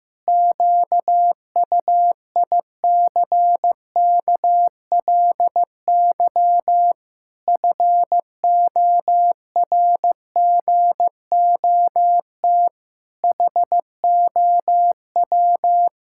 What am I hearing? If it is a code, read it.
Morse code, heard as a keyed tone, about 15 wpm, 700 Hz. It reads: QUICKLY FORGOT HOW
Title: Pangram Challenge v2 - 15wpm